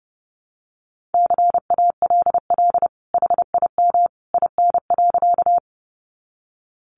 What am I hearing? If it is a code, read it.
Morse code, heard as a keyed tone, about 30 wpm, 700 Hz. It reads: CALL HIM IN.
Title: Sentences from Top 100 Words 30wpm